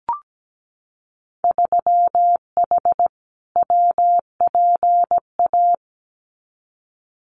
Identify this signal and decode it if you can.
Morse code, heard as a keyed tone, about 17 wpm, 700 Hz. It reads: 3H WPA
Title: Winter Field Day Exchanges 17wpm